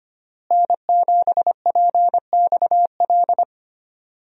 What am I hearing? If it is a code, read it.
Morse code, heard as a keyed tone, about 25 wpm, 700 Hz. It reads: N7PXL
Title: USA Call Signs - 5 Characters - Repeated 3x - 25wpm